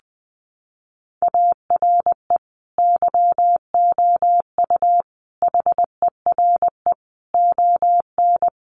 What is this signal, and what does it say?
Morse code, heard as a keyed tone, about 20 wpm, 700 Hz. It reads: ARE YOU HERE ON
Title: Sentences from Top 400 Words 20wpm